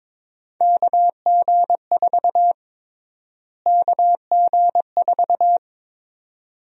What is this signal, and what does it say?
Morse code, heard as a keyed tone, about 22 wpm, 700 Hz. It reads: KG4 KG4
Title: USA Call Sign Prefixes - Repeated 2x - 22wpm